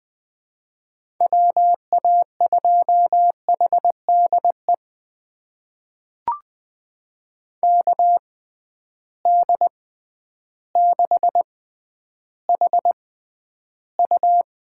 Morse code, heard as a keyed tone, about 20 wpm, spaced as for 11 wpm, 700 Hz. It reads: WA2HDE KD6HU
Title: USA Call Signs - 6 Characters - 6x Character Spacing - 20wpm